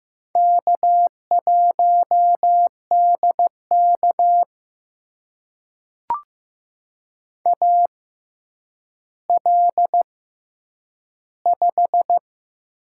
Morse code, heard as a keyed tone, about 15 wpm, spaced as for 9 wpm, 700 Hz. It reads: K1DK AL5
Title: USA Call Signs - 4 Characters - 6x Character Spacing - 15wpm